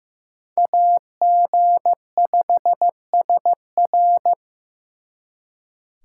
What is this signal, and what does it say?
Morse code, heard as a keyed tone, about 18 wpm, 700 Hz. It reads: AG5SR
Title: Speed Racing - USA Call Signs 15wpm